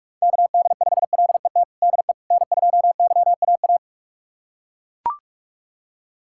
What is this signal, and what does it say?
Morse code, heard as a keyed tone, about 41 wpm, 700 Hz. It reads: KD5LET DE N2YAA
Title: USA Callsign DE Callsign 45wpm